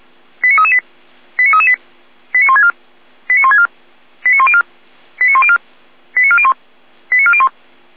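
Five-tone selective call (ZVEI), heard as quick runs of five tones, tones of about 70 ms. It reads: 89338 89338 89164 89164 89104 89104 89401 89401
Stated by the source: ZVEI five-tone selective call received on SDR.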